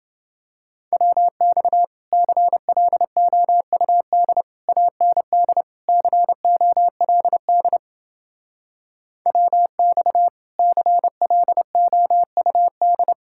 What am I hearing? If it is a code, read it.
Morse code, heard as a keyed tone, about 29 wpm, 700 Hz. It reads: WX CLOUD AND COLD WX CLOUD